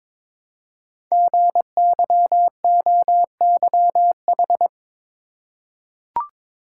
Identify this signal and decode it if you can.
Morse code, heard as a keyed tone, about 22 wpm, 700 Hz. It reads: GYOYH
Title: Code Groups - 5 Letter:Numbers - 22wpm